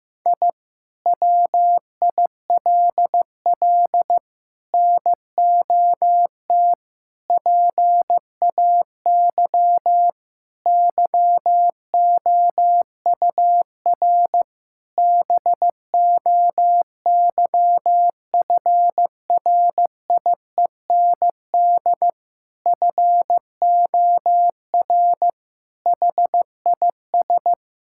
Morse code, heard as a keyed tone, about 15 wpm, 700 Hz. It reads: I WILL NOT PAY YOUR BOYFRIEND FOR HIS
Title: Sentences from Top 1000 Words - Review Part 5 - 15wpm